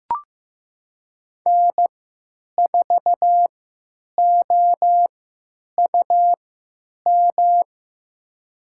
Morse code, heard as a keyed tone, about 15 wpm, spaced as for 8 wpm, 700 Hz. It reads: N4OUM
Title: USA Call Signs - 5 Characters - 3x Character Spacing - 15wpm